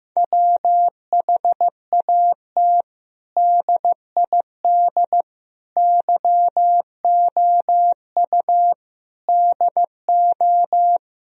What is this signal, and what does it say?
Morse code, heard as a keyed tone, about 15 wpm, 700 Hz. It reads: WHAT DID YOU DO
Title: Sentences from Top 1000 Words - Review Part 4 - 15wpm